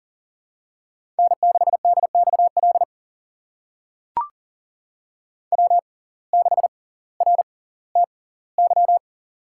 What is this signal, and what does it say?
Morse code, heard as a keyed tone, about 40 wpm, spaced as for 20 wpm, 700 Hz. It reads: N6DXL W6RTY
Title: USA Call Signs - 5 Characters - 6x Character Spacing - 40wpm